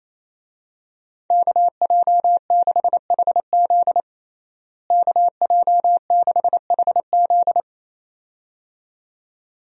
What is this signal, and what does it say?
Morse code, heard as a keyed tone, about 28 wpm, 700 Hz. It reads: KJ6HZ KJ6HZ
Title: USA Call Signs - 5 Characters - Repeated 2x - 28wpm